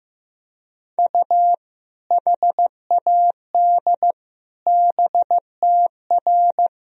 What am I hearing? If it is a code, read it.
Morse code, heard as a keyed tone, about 15 wpm, 700 Hz. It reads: U HAD BTR
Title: Sentences from Top 400 Words and Abbreviations 15wpm